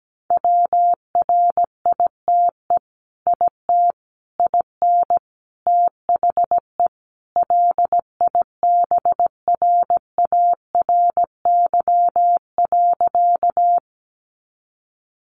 Morse code, heard as a keyed tone, about 17 wpm, 700 Hz. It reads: WRITE IT IN THE LIBRARY.
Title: Sentences from Top 600 Words 17wpm